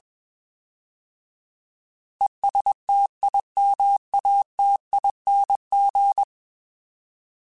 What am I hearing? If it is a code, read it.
Morse code, heard as a keyed tone, about 21 wpm, 800 Hz. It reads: ESTIMATING
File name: Alphabetical / Alphabetical mp3s / Es @20